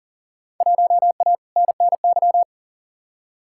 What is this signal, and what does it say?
Morse code, heard as a keyed tone, about 40 wpm, 700 Hz. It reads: 1A NNY
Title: Field Day Exchanges - 40wpm